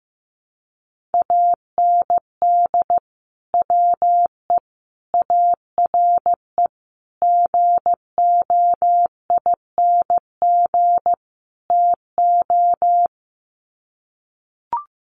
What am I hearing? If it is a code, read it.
Morse code, heard as a keyed tone, about 14 wpm, 700 Hz. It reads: AND WE ARE GOING TO
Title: Sets of 5 Words Encore2 15wpm